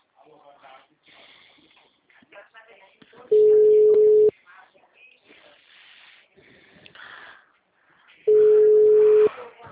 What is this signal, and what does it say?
Ringback tone (Europe, 425 Hz), with speech.
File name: out- -2017- - - .49027